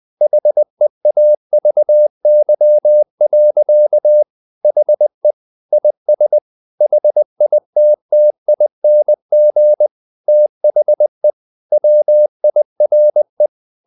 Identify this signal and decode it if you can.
Morse code, heard as a keyed tone, about 20 wpm, 600 Hz. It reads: HEAVY. HE IS HITTING THE WIRE
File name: the old man and the sea part 2 20wpm